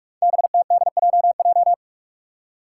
Morse code, heard as a keyed tone, about 46 wpm, 700 Hz. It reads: BTDJJ
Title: Five Challenging Letter-Number Combinations 45wpm